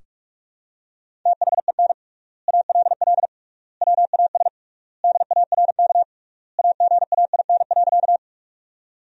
Morse code, heard as a keyed tone, about 46 wpm, 700 Hz. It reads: THEN ALL WAS DARK AGAIN.
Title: 3x Word-Spacing - Sentences from Top 500 Words - 45wpm